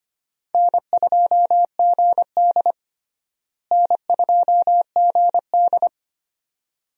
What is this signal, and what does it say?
Morse code, heard as a keyed tone, about 25 wpm, 700 Hz. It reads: N2GD N2GD
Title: USA Call Signs - 4 Characters - Repeated 2x - 25wpm